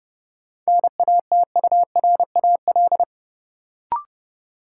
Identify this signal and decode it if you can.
Morse code, heard as a keyed tone, about 29 wpm, 700 Hz. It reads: NATURAL
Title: Sentences from Top 700 Words - New Word Drill - 30wpm